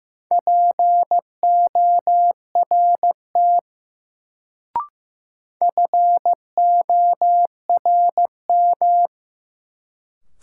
Morse code, heard as a keyed tone, about 15 wpm, 700 Hz. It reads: PORT FORM
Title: Sentences from Top 1000 Words - Review Part 1 - New Word Drill - 15wpm